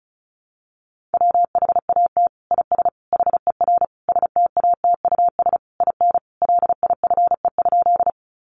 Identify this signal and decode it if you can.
Morse code, heard as a keyed tone, about 34 wpm, 700 Hz. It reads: WHAT IS HER STATUS IN LIFE?
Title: Sentences from Top 1400 Words 35wpm